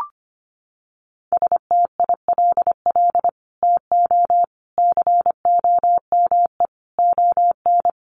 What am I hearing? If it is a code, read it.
Morse code, heard as a keyed tone, about 25 wpm, 700 Hz. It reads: STILL TO COME ON
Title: Sets of 5 Words Encore2 25wpm